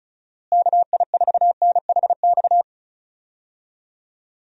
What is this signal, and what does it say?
Morse code, heard as a keyed tone, about 36 wpm, 700 Hz. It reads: KI4NHX
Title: USA Call Signs - 6 Characters - Repeated 2x - 35wpm